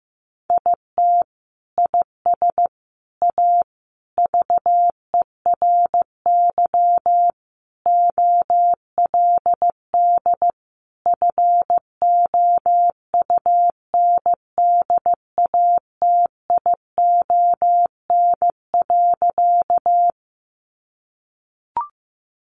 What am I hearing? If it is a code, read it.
Morse code, heard as a keyed tone, about 15 wpm, 700 Hz. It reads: IT IS A VERY OLD FOUNDATION.
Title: Sentences from Top 1200 Words 15wpm